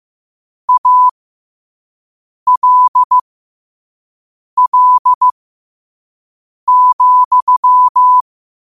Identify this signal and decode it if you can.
Morse code, heard as a keyed tone, about 15 wpm, spaced as for 5 wpm, 1000 Hz. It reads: ALL,
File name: qotdfiles / 5wpm